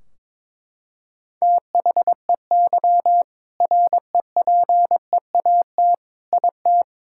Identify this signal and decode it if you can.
Morse code, heard as a keyed tone, about 21 wpm, 700 Hz. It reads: THEY REPEAT IT
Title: Sentences from Top 1600 Words 22wpm